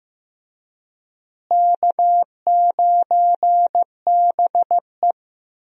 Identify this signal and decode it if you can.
Morse code, heard as a keyed tone, about 15 wpm, 700 Hz. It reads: K9BE
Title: USA Call Signs - 4 Characters - Repeated 3x - 15wpm